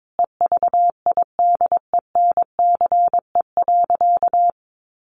Morse code, heard as a keyed tone, about 22 wpm, 700 Hz. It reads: EVIDENCE.
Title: Sentences from Top 1300 Words 22wpm